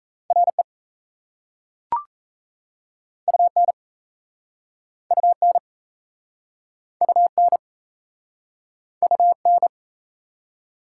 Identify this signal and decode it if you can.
Morse code, heard as a keyed tone, about 40 wpm, 700 Hz. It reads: RE UN UN UN UN